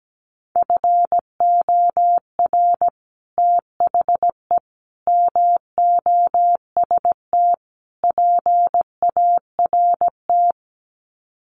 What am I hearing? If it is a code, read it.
Morse code, heard as a keyed tone, about 17 wpm, 700 Hz. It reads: FOR THE MOST PART
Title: Sets of 4 Words 17wpm